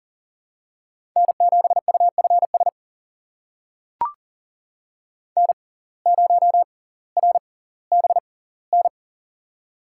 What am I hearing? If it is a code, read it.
Morse code, heard as a keyed tone, about 38 wpm, spaced as for 20 wpm, 700 Hz. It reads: N7UFS N0RBN
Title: USA Call Signs - 5 Characters - 6x Character Spacing - 40wpm